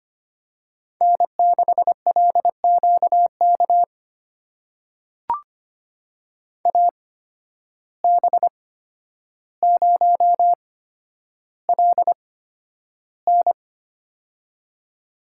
Morse code, heard as a keyed tone, about 25 wpm, spaced as for 11 wpm, 700 Hz. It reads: N6LQK AB0LN
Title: USA Call Signs - 5 Characters - 8x Character Spacing - 25wpm